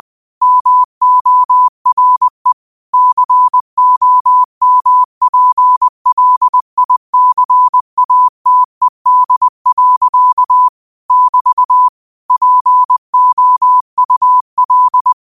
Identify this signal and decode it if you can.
Morse code, heard as a keyed tone, about 20 wpm, 1000 Hz. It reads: MORE COMPLICATED. <BT> POUL
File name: qotdfiles / 20wpm